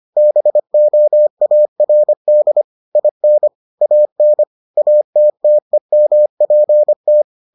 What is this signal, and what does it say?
Morse code, heard as a keyed tone, about 25 wpm, 600 Hz. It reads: BOARD IN AN ATTEMPT